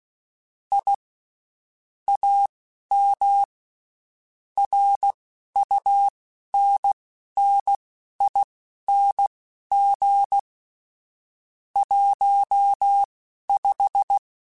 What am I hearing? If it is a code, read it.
Morse code, heard as a keyed tone, about 16 wpm, spaced as for 11 wpm, 800 Hz. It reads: I AM RUNNING 15
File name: (spaced) / QSOs Spaced-mp3s / WD0BGP 1 @15 sp